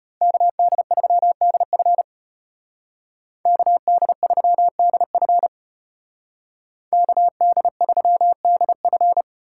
Morse code, heard as a keyed tone, about 30 wpm, 700 Hz. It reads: KD3DF KD3DF KD3DF
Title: Speed Racing - USA 2x2 Call Signs 25wpm